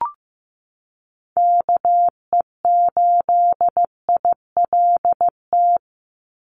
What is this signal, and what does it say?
Morse code, heard as a keyed tone, about 15 wpm, 700 Hz. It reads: KE8ILT
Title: USA Call Signs 15wpm